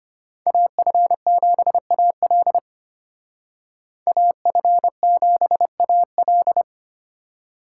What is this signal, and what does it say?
Morse code, heard as a keyed tone, about 33 wpm, 700 Hz. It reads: AF7AL AF7AL